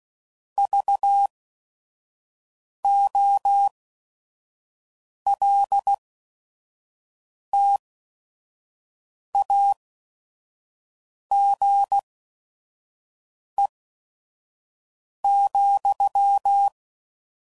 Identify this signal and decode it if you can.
Morse code, heard as a keyed tone, about 16 wpm, spaced as for 5 wpm, 800 Hz. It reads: VOLTAGE,
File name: CW Code Groups @5WPM